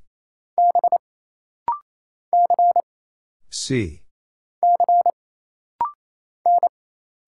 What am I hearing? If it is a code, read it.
Morse code, heard as a keyed tone, about 27 wpm, 700 Hz. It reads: B C C N
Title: Single Letters - Rapid-Fire 28wpm